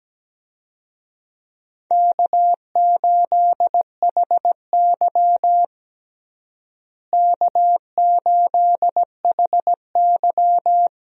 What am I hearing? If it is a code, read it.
Morse code, heard as a keyed tone, about 17 wpm, 700 Hz. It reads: K8HY K8HY
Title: USA Call Signs - 4 Characters - Repeated 2x - 17wpm